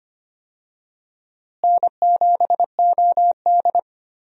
Morse code, heard as a keyed tone, about 25 wpm, 700 Hz. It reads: N7OD
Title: USA Call Signs - 4 Characters - Repeated 3x - 25wpm